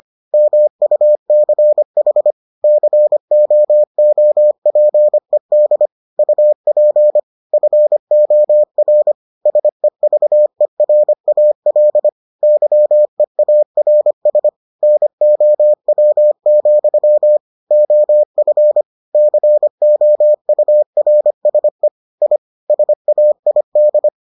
Morse code, heard as a keyed tone, about 25 wpm, 600 Hz. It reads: MUCH COOPED UP FOR SEVERAL YEARS NOW, OF COURSE I SAID